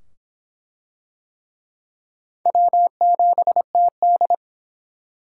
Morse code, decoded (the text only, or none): W7TD